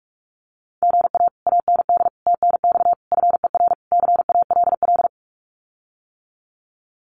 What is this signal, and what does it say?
Morse code, heard as a keyed tone, about 46 wpm, 700 Hz. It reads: GA AND TNX FER CALL
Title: Common QSO Phrases Encore 45wpm